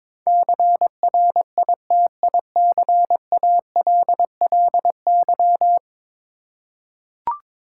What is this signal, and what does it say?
Morse code, heard as a keyed tone, about 22 wpm, 700 Hz. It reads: CRITICALLY
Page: morse-code-ninja-practice-22wpm